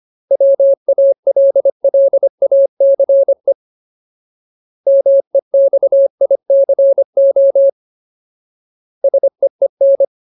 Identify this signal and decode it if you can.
Morse code, heard as a keyed tone, about 25 wpm, 550 Hz. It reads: WALLACE MEXICO SEEN
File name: Ultimate Word List / 25WPM Ultimate